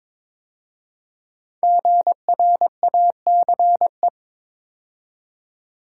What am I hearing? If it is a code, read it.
Morse code, heard as a keyed tone, about 22 wpm, 700 Hz. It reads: GRACE
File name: 5 Letter Words - First 3 Letters Given - 22wpm